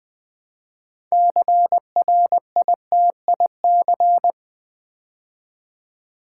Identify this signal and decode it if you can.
Morse code, heard as a keyed tone, about 20 wpm, 700 Hz. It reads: CRITIC
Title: Build A Word - 3 Letters At A Time - 6-Letter Words - 20wpm